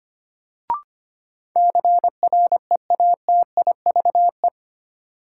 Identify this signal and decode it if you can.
Morse code, heard as a keyed tone, about 25 wpm, 700 Hz. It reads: CREATIVE
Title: Sentences from Top 1000 Words - Review Part 5 - New Word Drill - 25wpm